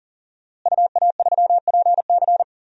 Morse code, heard as a keyed tone, about 40 wpm, 700 Hz. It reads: UA3PC
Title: DXCC-100 Call Signs - 5 Characters - Repeated 3x - 40wpm